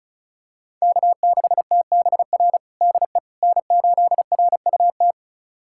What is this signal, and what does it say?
Morse code, heard as a keyed tone, about 36 wpm, 700 Hz. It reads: K6TBR DE N8RUT
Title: USA Callsign DE Callsign - 5 Characters - 35wpm